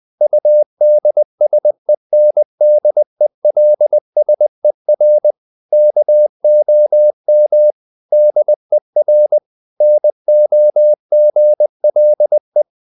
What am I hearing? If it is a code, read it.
Morse code, heard as a keyed tone, about 20 wpm, 600 Hz. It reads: UDSENDELSER KOM DER NOGLE
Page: Ferie 20 wpm